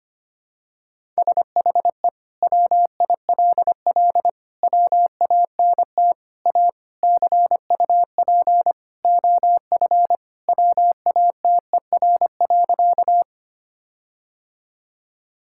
Morse code, heard as a keyed tone, about 25 wpm, 700 Hz. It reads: SHE WILL WANT A CUP OF WATER.
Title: Sentences from Top 1600 Words 25wpm